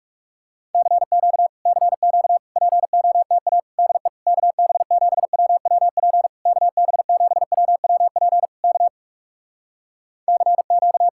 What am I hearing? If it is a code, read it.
Morse code, heard as a keyed tone, about 44 wpm, 700 Hz. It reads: CQ CQ POTA DE KB7WWP KB7WWP K CQ